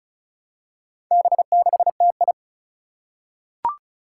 Morse code, heard as a keyed tone, about 36 wpm, 700 Hz. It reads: B6TI
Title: Four Challenging Letter-Number Combinations 35wpm